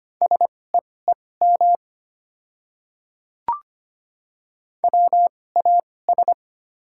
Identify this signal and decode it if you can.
Morse code, heard as a keyed tone, about 25 wpm, spaced as for 18 wpm, 690 Hz. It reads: SEEM WAS
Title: 2x Character Spacing - Top 300 Words - 25wpm